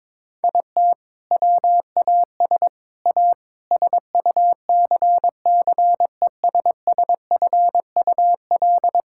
Morse code, heard as a keyed tone, about 22 wpm, 700 Hz. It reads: IT WAS A SUCCESSFUL
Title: Sentences from Top 1000 Words - Review Part 5 - 22wpm